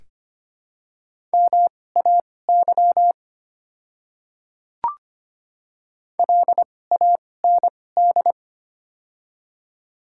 Morse code, heard as a keyed tone, about 25 wpm, spaced as for 18 wpm, 700 Hz. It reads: MAY LAND
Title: 2x Character Spacing - Top 300 Words - 25wpm